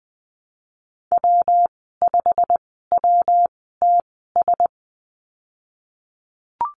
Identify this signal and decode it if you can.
Morse code, heard as a keyed tone, about 20 wpm, spaced as for 15 wpm, 700 Hz. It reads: W5WTS
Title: 122 Lesson 20g - Number 3 - Call Signs 20wpm